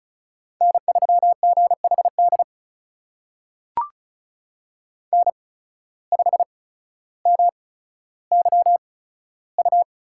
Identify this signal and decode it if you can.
Morse code, heard as a keyed tone, about 36 wpm, spaced as for 15 wpm, 700 Hz. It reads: N3GHD N5MYU
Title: USA Call Signs - 5 Characters - 8x Character Spacing - 35wpm